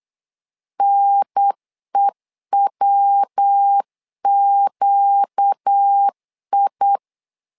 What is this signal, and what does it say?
Morse code, heard as a keyed tone, about 8 wpm, 800 Hz. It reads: NEWQI
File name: LEZIONI / 4 Q 8wpm 2732